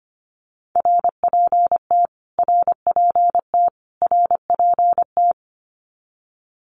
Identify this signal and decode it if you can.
Morse code, heard as a keyed tone, about 25 wpm, 700 Hz. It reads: RPT RPT RPT
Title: Abbreviations Repeated 3x - 25wpm